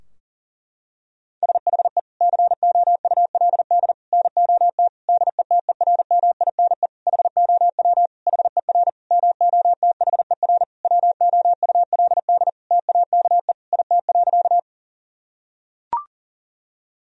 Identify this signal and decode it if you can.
Morse code, heard as a keyed tone, about 40 wpm, 700 Hz. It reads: SHE COULD NOT DETERMINE HOW HER MOTHER WOULD TAKE IT.
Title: Sentences from Top 1000 Words - Review Part 4 - 40wpm